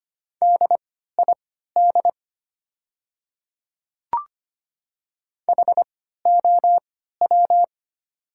Morse code, heard as a keyed tone, about 25 wpm, spaced as for 14 wpm, 700 Hz. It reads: DID HOW
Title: 3x Character Spacing - Top 100 Words - 25wpm